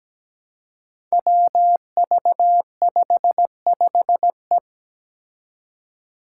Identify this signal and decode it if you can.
Morse code, heard as a keyed tone, about 17 wpm, 700 Hz. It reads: WV55E
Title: Five Challenging Letter-Number Combinations 17wpm